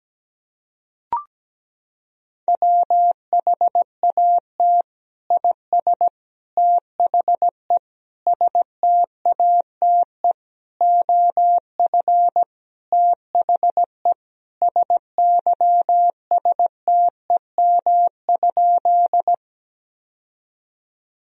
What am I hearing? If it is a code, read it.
Morse code, heard as a keyed tone, about 17 wpm, 700 Hz. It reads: WHAT IS THE STATE OF THE SYSTEM?
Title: Sentences from Top 1000 Words - Review Part 3 - 17wpm